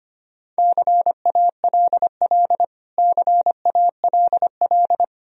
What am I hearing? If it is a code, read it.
Morse code, heard as a keyed tone, about 25 wpm, 700 Hz. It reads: CALL CALL
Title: Top 100 Words Repeated 25wpm